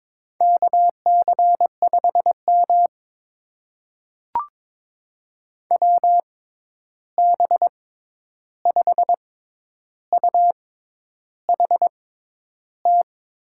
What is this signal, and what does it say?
Morse code, heard as a keyed tone, about 21 wpm, spaced as for 10 wpm, 700 Hz. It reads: KC5M WB5UHT
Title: USA Call Signs - 6x Character Spacing - 22wpm